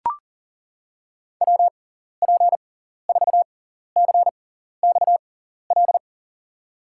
Morse code, heard as a keyed tone, about 37 wpm, spaced as for 14 wpm, 700 Hz. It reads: WP4CXL
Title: USA Call Signs - 6 Characters - 6x Character Spacing - 40wpm